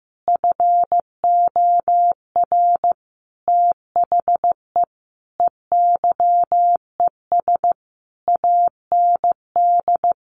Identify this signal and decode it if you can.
Morse code, heard as a keyed tone, about 15 wpm, 700 Hz. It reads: FOR THE EYES AND
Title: Easy Phrases 15wpm